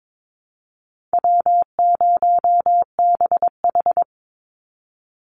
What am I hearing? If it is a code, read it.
Morse code, heard as a keyed tone, about 22 wpm, 700 Hz. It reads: W0BH
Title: Call Signs - 22wpm